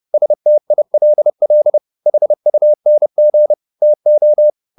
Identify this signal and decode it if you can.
Morse code, heard as a keyed tone, about 30 wpm, 600 Hz. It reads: STILL HUNG TO